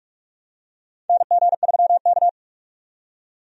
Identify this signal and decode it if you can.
Morse code, heard as a keyed tone, about 46 wpm, 700 Hz. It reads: NG3K
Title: USA Call Signs - 4 Characters - 45wpm